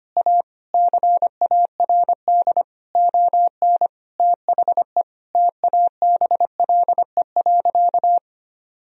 Morse code, heard as a keyed tone, about 25 wpm, 700 Hz. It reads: A CARD ON THE TABLE.
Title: Sentences from Top 1000 Words - Review Part 4 - 25wpm